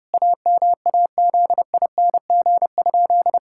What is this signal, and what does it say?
Morse code, heard as a keyed tone, about 30 wpm, 700 Hz. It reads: AMAZING?
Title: Pangram Challenge v2 - 30wpm